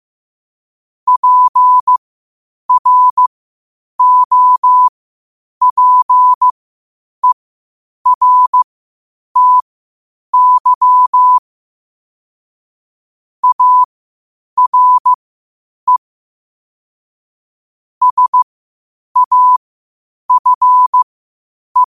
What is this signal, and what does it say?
Morse code, heard as a keyed tone, about 15 wpm, spaced as for 8 wpm, 1000 Hz. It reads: PROPERTY ARE SAFE
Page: New quotes every day in morse code at 8 Words per minute.